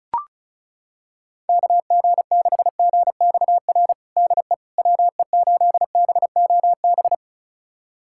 Morse code, heard as a keyed tone, about 36 wpm, 700 Hz. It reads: KG6GXR DE WE8BOB
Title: USA Callsign DE Callsign 35wpm